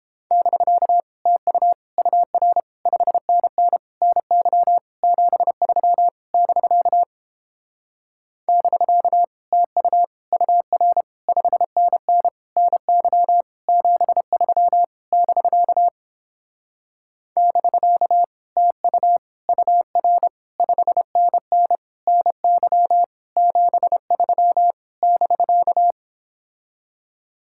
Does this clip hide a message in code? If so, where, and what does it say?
Yes: Morse code from 0.3 to 25.9 s, heard as a keyed tone, about 29 wpm, 700 Hz. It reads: <BK> TU UR 5NN NY 73 <BK> <BK> TU UR 5NN NY 73 <BK> <BK> TU UR 5NN NY 73 <BK>